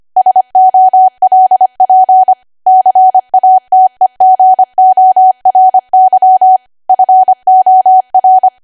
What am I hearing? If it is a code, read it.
Morse code, heard as a keyed tone, about 25 wpm, 750 Hz. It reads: SOLP CATEGORY FOR
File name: cwa / W1AW-Level3 / 25WPM